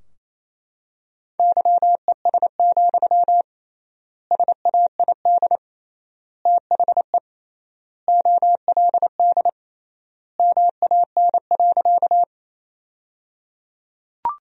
Morse code, heard as a keyed tone, about 28 wpm, 700 Hz. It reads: YES, SAID THE OLD MAN.
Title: 3x Word-Spacing - Sentences from Top 500 Words - 28wpm